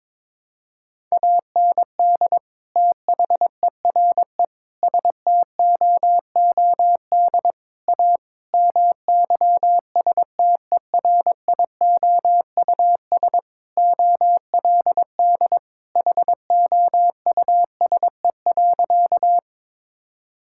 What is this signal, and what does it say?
Morse code, heard as a keyed tone, about 22 wpm, 700 Hz. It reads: AND THERE STOOD A MYSTERIOUS OLD HOUSE.
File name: practice / 22wpm / Sentences with 2-Word Phrases - 22wpm